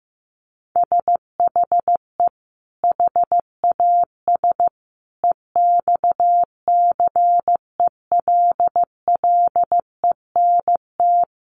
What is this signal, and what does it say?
Morse code, heard as a keyed tone, about 15 wpm, 700 Hz. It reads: SHE HAS EXCELLENT
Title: Sentences from Top 1100 Words 15wpm